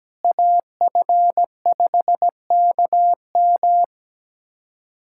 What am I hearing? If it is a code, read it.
Morse code, heard as a keyed tone, about 17 wpm, 700 Hz. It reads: AF5KM
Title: USA Callsign DE Callsign 17wpm